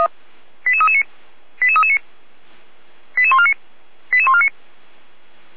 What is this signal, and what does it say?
Five-tone selective call (ZVEI), heard as quick runs of five tones, tones of about 70 ms. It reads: 88339 88339 88149 88149
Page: audio.f85360.feuerwehr-orlamuende.org